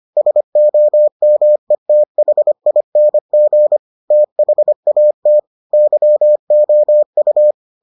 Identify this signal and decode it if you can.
Morse code, heard as a keyed tone, about 25 wpm, 600 Hz. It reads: SOMETHING THAT YOU